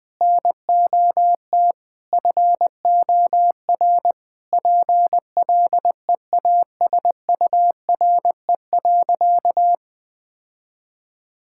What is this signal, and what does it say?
Morse code, heard as a keyed tone, about 20 wpm, 700 Hz. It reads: NOT FOR PLEASURE.
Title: Sentences from Top 1600 Words 20wpm